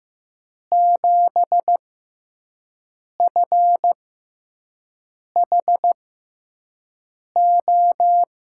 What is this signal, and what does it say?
Morse code, heard as a keyed tone, about 15 wpm, spaced as for 5 wpm, 700 Hz. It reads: 7FHO
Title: USA Call Signs - 5 Characters - 6x Character Spacing - 15wpm